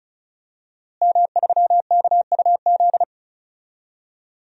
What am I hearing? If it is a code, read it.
Morse code, heard as a keyed tone, about 36 wpm, 700 Hz. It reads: M3KUZ
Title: DXCC-100 Call Signs - 5 Characters - Repeated 2x - 35wpm